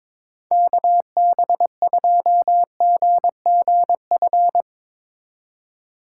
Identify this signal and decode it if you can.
Morse code, heard as a keyed tone, about 23 wpm, 700 Hz. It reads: KB2GGF
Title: Speed Racing - USA 2x3 Call Signs 22wpm